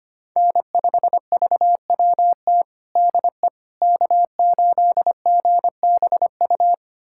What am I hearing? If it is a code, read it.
Morse code, heard as a keyed tone, about 25 wpm, 700 Hz. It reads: N5VWT DE K8GBU
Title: USA Callsign DE Callsign - 5 Characters - 25wpm